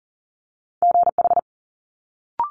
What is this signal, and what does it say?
Morse code, heard as a keyed tone, about 42 wpm, 700 Hz. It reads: GH
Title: Multi-letter Phonemes Quick Fire 40wpm